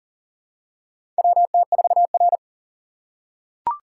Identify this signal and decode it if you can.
Morse code, heard as a keyed tone, about 40 wpm, 700 Hz. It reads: WT4R
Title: USA Call Signs - 4 Characters - 6x Character Spacing - 40wpm